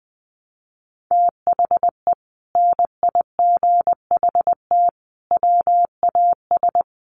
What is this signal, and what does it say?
Morse code, heard as a keyed tone, about 20 wpm, 700 Hz. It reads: THE NIGHT WAS
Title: Sentences from Top 400 Words 20wpm